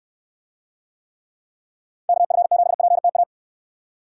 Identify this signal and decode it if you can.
Morse code, heard as a keyed tone, about 67 wpm, 700 Hz. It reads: KF7PTA